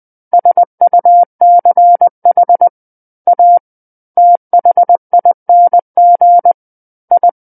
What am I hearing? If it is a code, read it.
Morse code, heard as a keyed tone, about 20 wpm, 700 Hz. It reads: SUCH A THING I
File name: 20 WPM